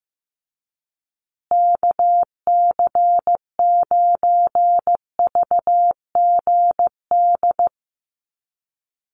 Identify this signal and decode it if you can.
Morse code, heard as a keyed tone, about 15 wpm, 700 Hz. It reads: KC9VGD
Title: USA Call Signs Repeated 3x 15wpm